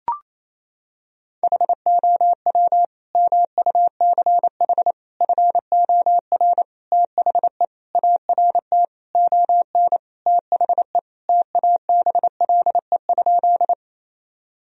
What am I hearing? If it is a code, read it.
Morse code, heard as a keyed tone, about 28 wpm, 700 Hz. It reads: HOW MUCH FOR THE ART ON THE TABLE?
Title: Sentences from Top 1000 Words - Review Part 3 - 28wpm